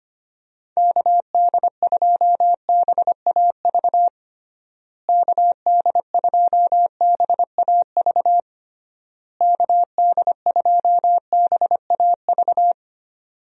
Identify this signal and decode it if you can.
Morse code, heard as a keyed tone, about 25 wpm, 700 Hz. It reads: KD2BAV KD2BAV KD2BAV
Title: USA Call Signs - 6 Characters - Repeated 3x - 25wpm